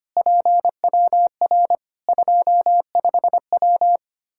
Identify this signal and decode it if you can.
Morse code, heard as a keyed tone, about 25 wpm, 700 Hz. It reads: PWR 25W
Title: Practice QSO - Exchange 2 - Info - 25wpm